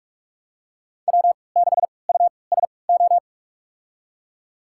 Morse code, heard as a keyed tone, about 46 wpm, spaced as for 26 wpm, 700 Hz. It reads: W6USO
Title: USA Call Signs - 5 Characters - 3x Character Spacing - 45wpm